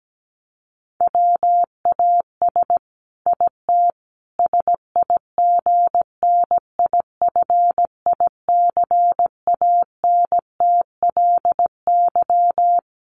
Morse code, heard as a keyed tone, about 17 wpm, 700 Hz. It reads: WAS IT SIGNIFICANTLY
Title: Sentences from Top 1500 Words 17wpm